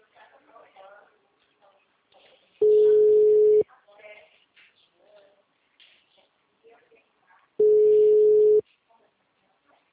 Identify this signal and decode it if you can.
Ringback tone (Europe, 425 Hz), with speech.